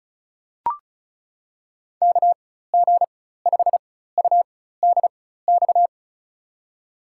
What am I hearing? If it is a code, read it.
Morse code, heard as a keyed tone, about 34 wpm, spaced as for 16 wpm, 700 Hz. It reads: KG5UDX
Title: USA Call Signs - 4x Character Spacing - 35wpm